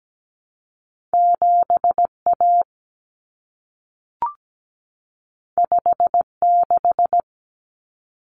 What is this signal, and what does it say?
Morse code, heard as a keyed tone, about 17 wpm, 700 Hz. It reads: 7A 56
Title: Common Cut Numbers - 2-Digit 17wpm